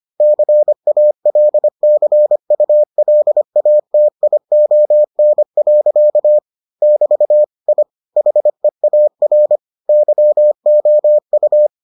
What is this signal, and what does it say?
Morse code, heard as a keyed tone, about 25 wpm, 600 Hz. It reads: CALCULATION. <BT> I HEAR YOU